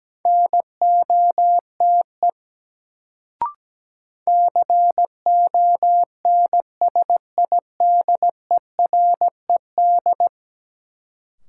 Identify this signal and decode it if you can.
Morse code, heard as a keyed tone, about 17 wpm, 700 Hz. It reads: NOTE CONSIDERED
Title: Sets of 5 Words - New Word Drill - 17wpm